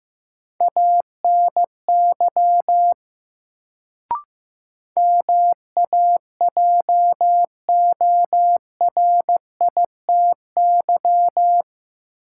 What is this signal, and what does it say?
Morse code, heard as a keyed tone, about 15 wpm, 700 Hz. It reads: ANY MAJORITY
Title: Sets of 3 Words - New Word Drill - 15wpm